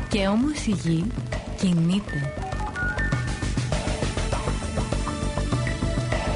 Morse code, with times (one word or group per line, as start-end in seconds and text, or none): none